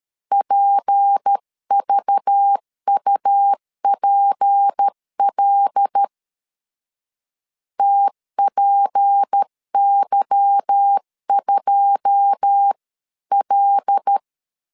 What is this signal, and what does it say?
Morse code, heard as a keyed tone, about 13 wpm, 800 Hz. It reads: PVUPL TPY2L
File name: LEZIONI / 9 28 12wpm 2351